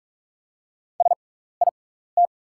Morse code, heard as a keyed tone, about 46 wpm, spaced as for 16 wpm, 690 Hz. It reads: SIT
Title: 6x Character Spacing - Top 500 Words - 45wpm